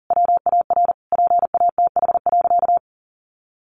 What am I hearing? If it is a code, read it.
Morse code, heard as a keyed tone, about 40 wpm, 700 Hz. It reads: WAR PATH.
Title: Sentences from Top 1400 Words 40wpm